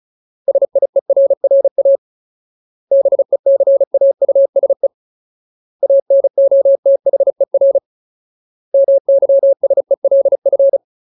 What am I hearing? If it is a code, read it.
Morse code, heard as a keyed tone, about 36 wpm, 550 Hz. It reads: SIERRA BECAUSE ANOTHER MYSELF
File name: Ultimate Word List / 35WPM Ultimate